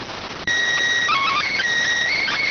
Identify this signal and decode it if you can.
SSTV (slow-scan television), PD 290.